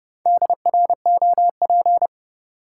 Morse code, heard as a keyed tone, about 30 wpm, 700 Hz. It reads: DROP
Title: Sentences from Top 1400 Words - New Word Drill Alt - 30wpm